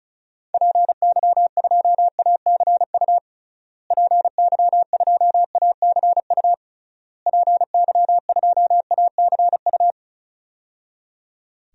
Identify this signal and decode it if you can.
Morse code, heard as a keyed tone, about 36 wpm, 700 Hz. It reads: PY2ACU PY2ACU PY2ACU
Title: DXCC-100 Call Signs - 6 Characters - Repeated 3x - 35wpm